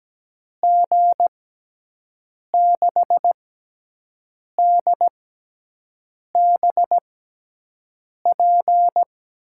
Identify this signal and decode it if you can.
Morse code, heard as a keyed tone, about 17 wpm, spaced as for 6 wpm, 700 Hz. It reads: G6DBP
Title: USA Call Signs - 6x Character Spacing - 17wpm